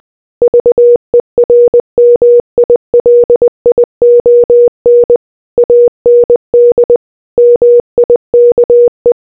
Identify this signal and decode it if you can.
Morse code, heard as a keyed tone, about 20 wpm, 480 Hz. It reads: VERMILION AND MIKE